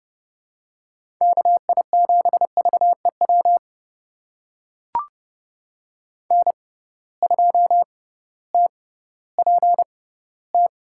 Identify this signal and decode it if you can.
Morse code, heard as a keyed tone, about 30 wpm, spaced as for 16 wpm, 700 Hz. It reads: KI7VEW N2TPT
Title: USA Call Signs - 6x Character Spacing - 30wpm